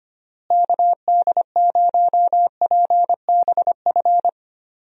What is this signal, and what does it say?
Morse code, heard as a keyed tone, about 25 wpm, 700 Hz. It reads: KD0PBF
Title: USA Call Signs - 6 Characters - Repeated 3x - 25wpm